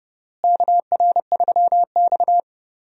Morse code, heard as a keyed tone, about 31 wpm, 700 Hz. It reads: KR3X
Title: USA Call Signs - 4 Characters - Repeated 2x - 30wpm